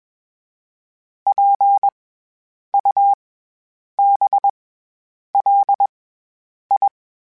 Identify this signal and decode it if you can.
Morse code, heard as a keyed tone, about 21 wpm, 800 Hz. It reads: P U B L I
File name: Special INGs-mp3s / Special INGs-mp3s / SlowToFaster ing @20